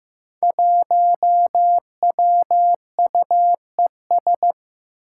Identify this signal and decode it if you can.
Morse code, heard as a keyed tone, about 15 wpm, 700 Hz. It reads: 1WUES
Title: Five Challenging Letter-Number Combinations 15wpm